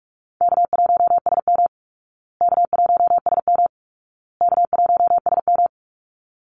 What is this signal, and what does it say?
Morse code, heard as a keyed tone, about 44 wpm, 700 Hz. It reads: K1SM K1SM K1SM
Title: Easy Call Signs Repeated 3x - 45wpm